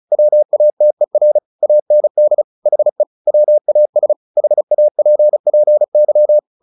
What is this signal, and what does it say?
Morse code, heard as a keyed tone, about 35 wpm, 600 Hz. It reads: WATER AND HE WAS HAPPY
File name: the old man and the sea / the old man and the sea 1 35wpm